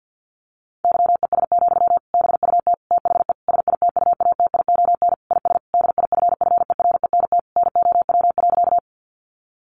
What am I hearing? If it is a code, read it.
Morse code, heard as a keyed tone, about 50 wpm, 700 Hz. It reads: YES, BUT THE SITUATION IS DIFFERENT NOW.
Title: Sentences from Top 1200 Words 50wpm